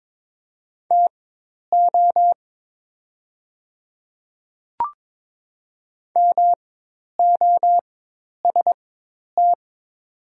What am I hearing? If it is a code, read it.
Morse code, heard as a keyed tone, about 22 wpm, spaced as for 10 wpm, 700 Hz. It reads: TO MOST
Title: 4x Character Spacing - Top 100 Words - 22wpm